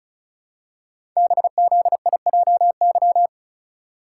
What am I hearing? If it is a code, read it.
Morse code, heard as a keyed tone, about 36 wpm, 700 Hz. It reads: BZIJY
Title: Code Groups - 5 Letters - 35wpm